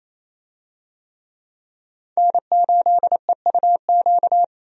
Morse code, heard as a keyed tone, about 28 wpm, 700 Hz. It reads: N8EUQ
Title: USA Call Signs - 5 Characters - 28wpm